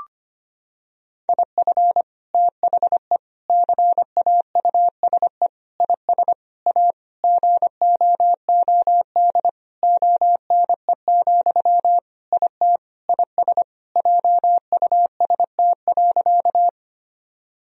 Morse code, heard as a keyed tone, about 25 wpm, 700 Hz. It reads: IF THE CAUSE IS A GOOD ONE, IT IS JUST.
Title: Sentences from Top 1000 Words - Review Part 1 - 25wpm